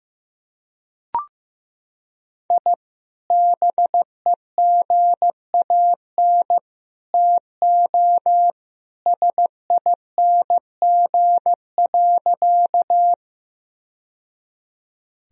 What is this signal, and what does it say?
Morse code, heard as a keyed tone, about 15 wpm, 700 Hz. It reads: I BEGAN TO SING.
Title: Sentences from Top 1000 Words - Review Part 2 - 15wpm